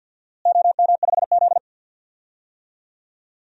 Morse code, heard as a keyed tone, about 51 wpm, 700 Hz. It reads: OM5Z
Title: DXCC-100 Call Signs - 4 Characters - Repeated 3x - 50wpm